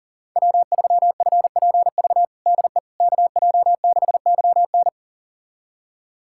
Morse code, heard as a keyed tone, about 40 wpm, 700 Hz. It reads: W3FPV DE KJ6YN
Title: USA Callsign DE Callsign - 5 Characters - 40wpm